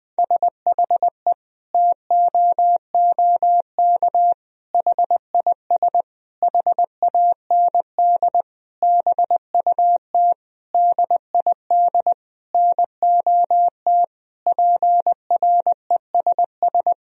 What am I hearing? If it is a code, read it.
Morse code, heard as a keyed tone, about 20 wpm, 700 Hz. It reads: SHE TOOK HIS HAND BUT DID NOT PRESS
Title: Sentences from Top 1000 Words - Review Part 1 - 20wpm